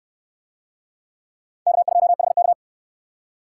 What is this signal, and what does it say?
Morse code, heard as a keyed tone, about 61 wpm, 700 Hz. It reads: K2HK